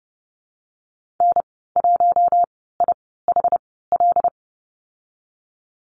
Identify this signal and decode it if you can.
Morse code, heard as a keyed tone, about 30 wpm, spaced as for 17 wpm, 700 Hz. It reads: N1IHL
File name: Lesson 14g - Number 2 - Call Signs 30wpm